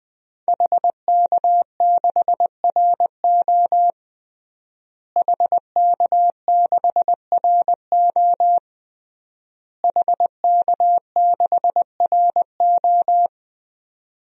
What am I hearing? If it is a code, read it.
Morse code, heard as a keyed tone, about 21 wpm, 700 Hz. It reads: HK6RO HK6RO HK6RO
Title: DXCC-100 Call Signs - 5 Characters - Repeated 3x - 20wpm